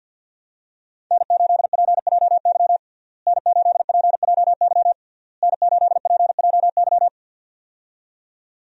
Morse code, heard as a keyed tone, about 52 wpm, 700 Hz. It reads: N8PJY N8PJY N8PJY
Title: USA Call Signs - 5 Characters - Repeated 3x - 50wpm